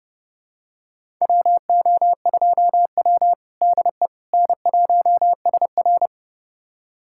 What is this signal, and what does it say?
Morse code, heard as a keyed tone, about 32 wpm, 700 Hz. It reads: WO2W DE N1SR
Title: USA Callsign DE Callsign - 4 Characters - 30wpm